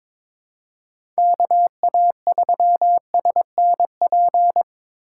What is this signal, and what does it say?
Morse code, heard as a keyed tone, about 25 wpm, 700 Hz. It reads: KA3SNP
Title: Speed Racing - USA 2x3 Call Signs 22wpm